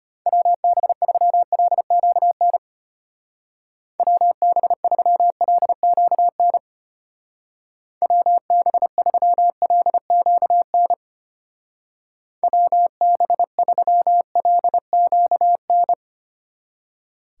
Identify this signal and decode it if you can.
Morse code, heard as a keyed tone, about 32 wpm, 700 Hz. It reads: WB3LQN WB3LQN WB3LQN WB3LQN